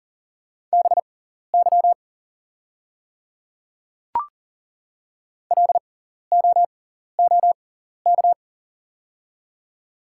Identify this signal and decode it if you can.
Morse code, heard as a keyed tone, about 40 wpm, spaced as for 14 wpm, 690 Hz. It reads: BY LOOK
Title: 6x Character Spacing - Top 100 Words - 40wpm